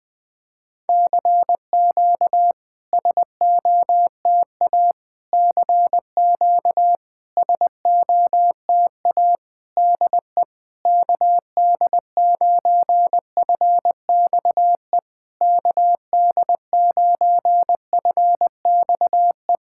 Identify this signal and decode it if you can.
Morse code, heard as a keyed tone, about 20 wpm, 700 Hz. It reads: CQ SOTA CQ SOTA DE KD9FXE KD9FXE
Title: SOTA QSO Practice 20wpm